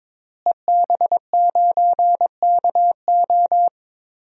Morse code, heard as a keyed tone, about 22 wpm, 700 Hz. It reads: EB9KO
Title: DXCC-100 Call Signs - 5 Characters - Repeated 3x - 22wpm